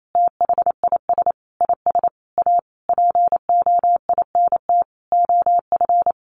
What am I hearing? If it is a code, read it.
Morse code, heard as a keyed tone, about 28 wpm, 700 Hz. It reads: THIS IS A POINT OF
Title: Sentences from Top 1500 Words 28wpm